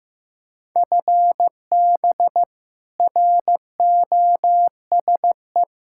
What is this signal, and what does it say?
Morse code, heard as a keyed tone, about 16 wpm, 700 Hz. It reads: FB ROSE
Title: Practice QSO - Exchange 3 - More Info - 15wpm